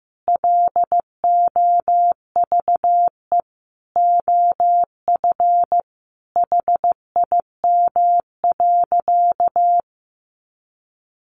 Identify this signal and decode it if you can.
Morse code, heard as a keyed tone, about 15 wpm, 700 Hz. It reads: LOVE OF HIM.
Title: Sentences from Top 1500 Words 15wpm